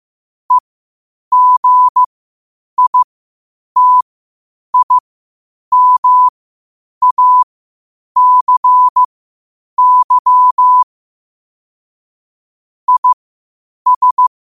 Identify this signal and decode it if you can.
Morse code, heard as a keyed tone, about 15 wpm, spaced as for 8 wpm, 1000 Hz. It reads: EGITIMACY IS
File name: qotdfiles / 8wpm